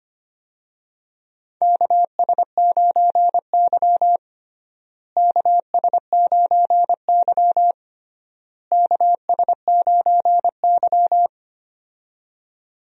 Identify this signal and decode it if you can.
Morse code, heard as a keyed tone, about 25 wpm, 700 Hz. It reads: KS9Y KS9Y KS9Y
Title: USA Call Signs - 4 Characters - Repeated 3x - 25wpm